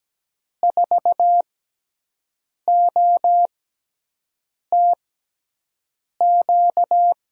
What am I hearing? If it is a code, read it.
Morse code, heard as a keyed tone, about 17 wpm, spaced as for 6 wpm, 700 Hz. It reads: 4OTQ
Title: USA Call Signs - 6 Characters - 6x Character Spacing - 17wpm